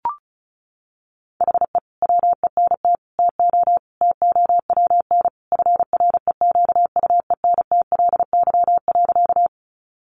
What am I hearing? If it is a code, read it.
Morse code, heard as a keyed tone, about 34 wpm, 700 Hz. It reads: HE WENT TO TOWN FREQUENTLY.
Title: Sentences from Top 1300 Words 35wpm